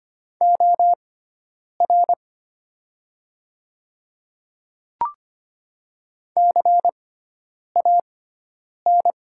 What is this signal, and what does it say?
Morse code, heard as a keyed tone, about 25 wpm, spaced as for 9 wpm, 700 Hz. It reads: OR CAN
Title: 6x Character Spacing - Top 100 Words - 25wpm